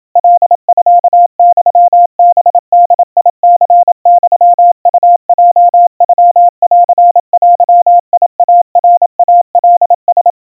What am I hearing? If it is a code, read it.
Morse code (Japanese Wabun code), heard as a keyed tone, about 27 wpm, 700 Hz. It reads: カミユハボニユウヲノンデイナイカラ
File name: 100 cpm 700 Hz Crispy